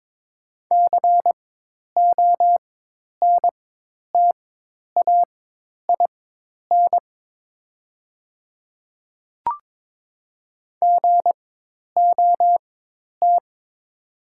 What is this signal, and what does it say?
Morse code, heard as a keyed tone, about 22 wpm, spaced as for 10 wpm, 690 Hz. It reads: CONTAIN GOT
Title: 4x Character Spacing - Top 500 Words - 22wpm